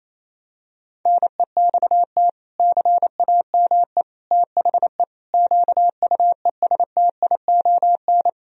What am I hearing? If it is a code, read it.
Morse code, heard as a keyed tone, about 28 wpm, 700 Hz. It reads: NEXT CAME THE QUESTION
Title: Sentences from Top 1000 Words - Review Part 4 - 28wpm